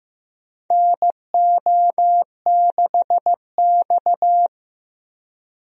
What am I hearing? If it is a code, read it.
Morse code, heard as a keyed tone, about 15 wpm, 700 Hz. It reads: NO6X
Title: USA Call Signs - 4 Characters - Repeated 3x - 15wpm